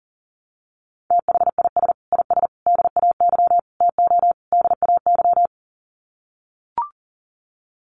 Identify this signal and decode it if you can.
Morse code, heard as a keyed tone, about 37 wpm, 700 Hz. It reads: THIS IS DAY TO DAY
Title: Sets of 5 Words Encore 40wpm